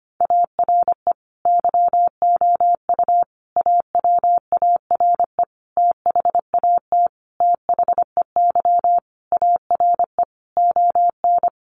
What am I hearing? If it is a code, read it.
Morse code, heard as a keyed tone, about 25 wpm, 700 Hz. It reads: ARE YOU AWARE THAT THEY ARE ON
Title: Sentences from Top 1200 Words 25wpm